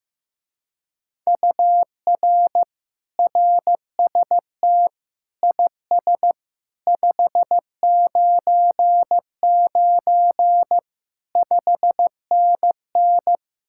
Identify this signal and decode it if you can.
Morse code, heard as a keyed tone, about 16 wpm, 700 Hz. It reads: UR RST IS 599 5NN
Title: Speed-Racing Common QSO Phrases Encore - 15wpm